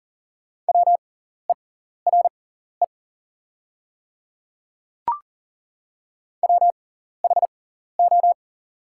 Morse code, heard as a keyed tone, about 40 wpm, spaced as for 14 wpm, 700 Hz. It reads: WERE WHO
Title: 6x Character Spacing - Top 100 Words - 40wpm